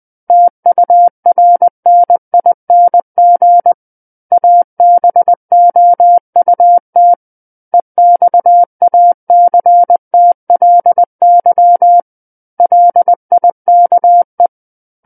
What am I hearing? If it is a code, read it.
Morse code, heard as a keyed tone, about 20 wpm, 700 Hz. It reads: TURNING ABOUT EXACTLY LIKE
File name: 20 WPM